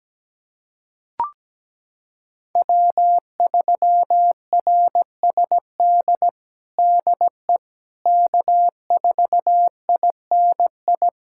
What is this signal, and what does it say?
Morse code, heard as a keyed tone, about 17 wpm, 700 Hz. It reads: W3RSD DE K4INI
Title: USA Callsign DE Callsign - 5 Characters - 17wpm